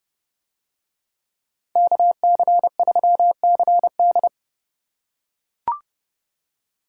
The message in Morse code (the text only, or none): KC3CD